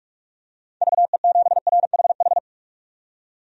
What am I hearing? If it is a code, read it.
Morse code, heard as a keyed tone, about 43 wpm, 700 Hz. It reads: VE7RHH
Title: DXCC-100 Call Signs - 6 Characters - 45wpm